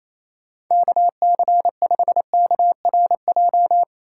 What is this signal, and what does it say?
Morse code, heard as a keyed tone, about 28 wpm, 690 Hz. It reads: KC5KRJ
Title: USA Call Signs - 6 Characters - 2x Character Spacing - 28wpm